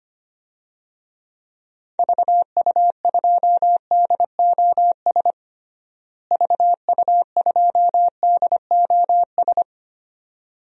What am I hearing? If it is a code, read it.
Morse code, heard as a keyed tone, about 25 wpm, 700 Hz. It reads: VU2DOS VU2DOS
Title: DXCC-100 Call Signs - 6 Characters - Repeated 2x - 25wpm